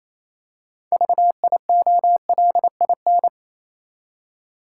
Morse code, heard as a keyed tone, about 27 wpm, 690 Hz. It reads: VIOLIN
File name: Easy Words - Speed-Racing with Hints - 22wpm